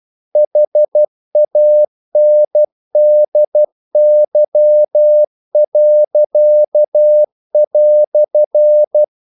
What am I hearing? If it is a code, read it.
Morse code, heard as a keyed tone, about 12 wpm, 600 Hz. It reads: HANDY."
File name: CW / Practice / EnglishQuotations0312